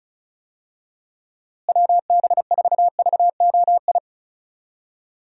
Morse code, heard as a keyed tone, about 34 wpm, 700 Hz. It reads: WB4VOI
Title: USA Call Signs - 6 Characters - Repeated 3x - 35wpm